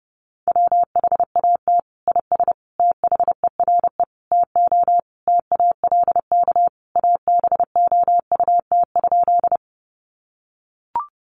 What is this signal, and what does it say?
Morse code, heard as a keyed tone, about 30 wpm, 700 Hz. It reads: WHAT IS THERE TO TALK ABOUT?
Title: Sentences from Top 400 Words 30wpm